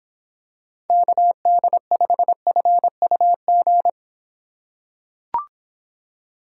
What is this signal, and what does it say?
Morse code, heard as a keyed tone, about 25 wpm, 700 Hz. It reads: KD5FUG